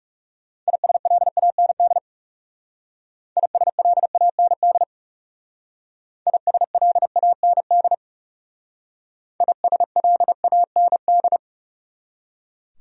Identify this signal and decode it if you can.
Morse code, heard as a keyed tone, about 37 wpm, 700 Hz. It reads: ISLAND ISLAND ISLAND ISLAND
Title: Speed-Racing Top 500 Words 30wpm